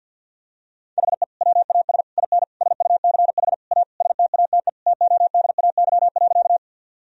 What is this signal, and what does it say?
Morse code, heard as a keyed tone, about 50 wpm, 700 Hz. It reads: HE WAS IN SUCH A STATE TODAY.
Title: Sentences from Top 1000 Words - Review Part 2 - 50wpm